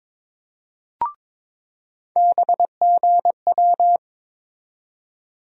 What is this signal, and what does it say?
Morse code, heard as a keyed tone, about 22 wpm, 700 Hz. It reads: BGW
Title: USA Call Sign Postfixes - 3 Letters 22wpm